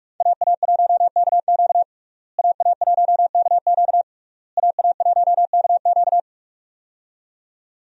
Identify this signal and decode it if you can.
Morse code, heard as a keyed tone, about 46 wpm, 700 Hz. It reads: AA1KQ AA1KQ AA1KQ
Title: USA Call Signs - 5 Characters - Repeated 3x - 45wpm